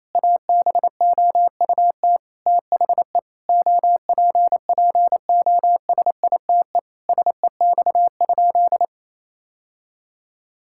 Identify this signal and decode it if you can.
Morse code, heard as a keyed tone, about 28 wpm, 700 Hz. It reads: ABOUT THE OPPOSITE SEX?
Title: Sentences from Top 1600 Words 28wpm